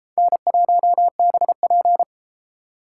Morse code, heard as a keyed tone, about 31 wpm, 700 Hz. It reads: N1BP